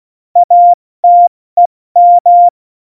Morse code, heard as a keyed tone, about 16 wpm, 700 Hz. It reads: ATEM
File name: Morsekurs / Witze / Witz 19. Dezember 2017, 70 F80